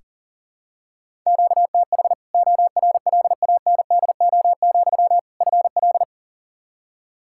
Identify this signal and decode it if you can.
Morse code, heard as a keyed tone, about 40 wpm, 700 Hz. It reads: QTH ORLANDO, FL
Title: Practice QSO - Exchange 1 - Beginning - 40wpm